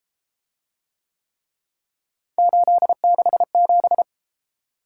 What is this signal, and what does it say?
Morse code, heard as a keyed tone, about 32 wpm, 700 Hz. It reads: 867